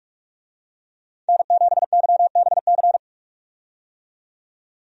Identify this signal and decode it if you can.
Morse code, heard as a keyed tone, about 42 wpm, 700 Hz. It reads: N7YBC
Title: USA Call Signs - 5 Characters - 45wpm